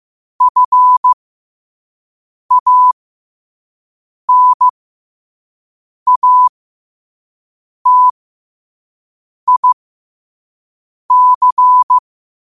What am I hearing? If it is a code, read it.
Morse code, heard as a keyed tone, about 15 wpm, spaced as for 5 wpm, 1000 Hz. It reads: FANATIC
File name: qotdfiles / 5wpm